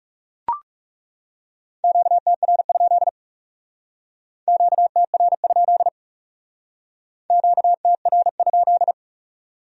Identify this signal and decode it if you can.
Morse code, heard as a keyed tone, about 37 wpm, 700 Hz. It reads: QTR? QTR? QTR?
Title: Speed-Racing Q-Codes - 30wpm